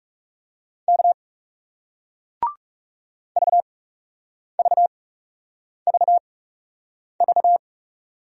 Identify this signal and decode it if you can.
Morse code, heard as a keyed tone, about 40 wpm, 700 Hz. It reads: K V V V V